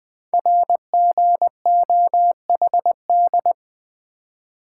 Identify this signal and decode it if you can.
Morse code, heard as a keyed tone, about 20 wpm, 700 Hz. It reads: RGOHD
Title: Code Groups - 5 Letters - 20wpm